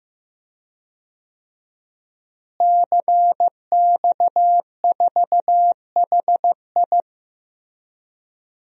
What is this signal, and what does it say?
Morse code, heard as a keyed tone, about 15 wpm, 700 Hz. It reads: CX4HI
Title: DXCC-100 Call Signs - 5 Characters - Repeated 2x - 15wpm